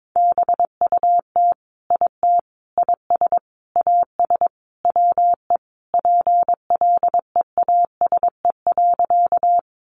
Morse code, heard as a keyed tone, about 22 wpm, 700 Hz. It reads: BUT IT IS AS WE PLEASE.
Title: Sentences from Top 1500 Words 22wpm